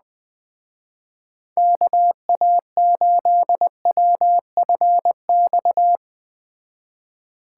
Morse code, heard as a keyed tone, about 20 wpm, 700 Hz. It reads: KA8WFX